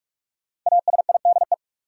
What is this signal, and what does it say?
Morse code, heard as a keyed tone, about 44 wpm, 700 Hz. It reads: ASIDE
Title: Five Letter Words - 45wpm